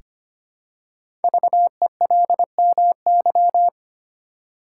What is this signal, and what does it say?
Morse code, heard as a keyed tone, about 25 wpm, 700 Hz. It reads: VELMY
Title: Code Groups - 5 Letters - 25wpm